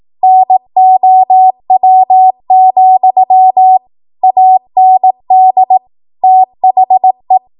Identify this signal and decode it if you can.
Morse code, heard as a keyed tone, about 18 wpm, 750 Hz. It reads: NOW, AND THE
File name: Morse / 18WPM